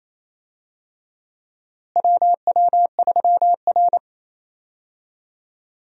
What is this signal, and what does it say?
Morse code, heard as a keyed tone, about 28 wpm, 700 Hz. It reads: WW3R
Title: USA Call Signs - 4 Characters - Repeated 2x - 28wpm